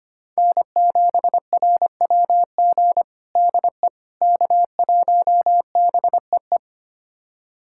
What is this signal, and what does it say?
Morse code, heard as a keyed tone, about 25 wpm, 700 Hz. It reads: N7RWG DE K1BEE
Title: USA Callsign DE Callsign - 5 Characters - 25wpm